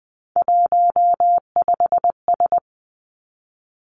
Morse code, heard as a keyed tone, about 19 wpm, 700 Hz. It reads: 15S
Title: Three Challenging Letter-Number Combinations 20wpm